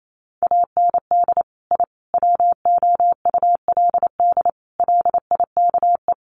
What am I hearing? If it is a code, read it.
Morse code, heard as a keyed tone, about 28 wpm, 700 Hz. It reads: AND I WOULD LIKE
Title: Sets of 5 Words Encore 28wpm